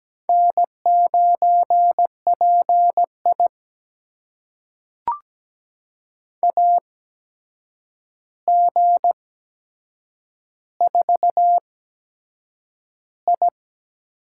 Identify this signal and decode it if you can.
Morse code, heard as a keyed tone, about 17 wpm, spaced as for 7 wpm, 700 Hz. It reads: N9PI AG4I
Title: USA Call Signs - 4 Characters - 8x Character Spacing - 17wpm